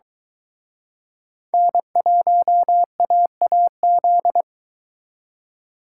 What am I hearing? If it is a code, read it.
Morse code, heard as a keyed tone, about 23 wpm, 700 Hz. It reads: N1AAZ